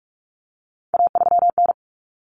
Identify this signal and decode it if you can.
Morse code, heard as a keyed tone, about 44 wpm, 700 Hz. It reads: A3N
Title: Common Cut Numbers - 3-Digit 45wpm